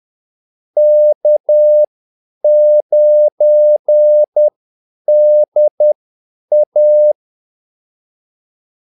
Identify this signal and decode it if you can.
Morse code, heard as a keyed tone, about 10 wpm, spaced as for 8 wpm, 600 Hz. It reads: K9DA
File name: Callsigns-10-8wpm